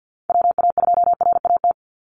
Morse code, heard as a keyed tone, about 50 wpm, 700 Hz. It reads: WA2RAT
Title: USA Call Signs 50wpm